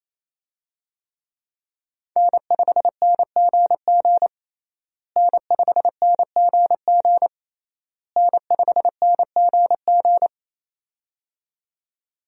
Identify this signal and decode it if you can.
Morse code, heard as a keyed tone, about 28 wpm, 700 Hz. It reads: N5NGG N5NGG N5NGG
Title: USA Call Signs - 5 Characters - Repeated 3x - 28wpm